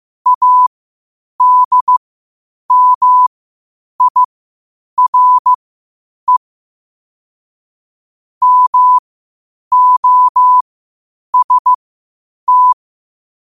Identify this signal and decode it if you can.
Morse code, heard as a keyed tone, about 15 wpm, spaced as for 8 wpm, 1000 Hz. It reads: ADMIRE MOST
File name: qotdfiles / 8wpm